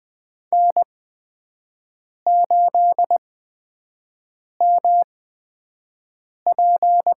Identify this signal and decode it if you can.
Morse code, heard as a keyed tone, about 20 wpm, spaced as for 5 wpm, 700 Hz. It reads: N8MP
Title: USA Call Signs - 4 Characters - 8x Character Spacing - 20wpm